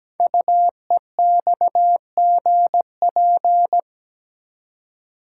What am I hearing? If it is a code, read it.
Morse code, heard as a keyed tone, about 17 wpm, 700 Hz. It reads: UEXGP
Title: Code Groups - 5 Letters - 17wpm